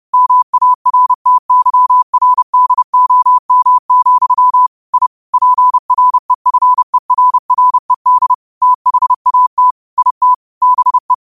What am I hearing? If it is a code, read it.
Morse code, heard as a keyed tone, about 30 wpm, 1000 Hz. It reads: MARTYRDOM, I PREFERRED THAT IT BE